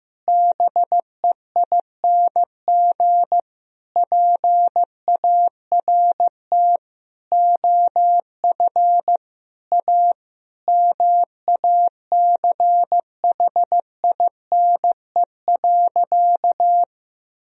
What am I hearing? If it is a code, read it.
Morse code, heard as a keyed tone, about 15 wpm, 700 Hz. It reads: BEING PART OF A MACHINE.
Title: Sentences from Top 1000 Words - Review Part 3 - 15wpm